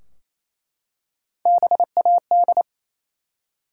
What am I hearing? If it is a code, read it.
Morse code, heard as a keyed tone, about 28 wpm, 700 Hz. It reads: BAD
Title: Three Letter Words - 28wpm